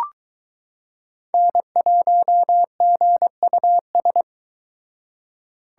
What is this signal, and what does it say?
Morse code, heard as a keyed tone, about 22 wpm, 700 Hz. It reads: N1GUS